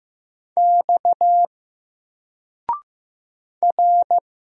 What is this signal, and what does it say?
Morse code, heard as a keyed tone, about 15 wpm, 700 Hz. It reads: X R
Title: Single Character (Letter:Number:Punctuation:Prosign) 15wpm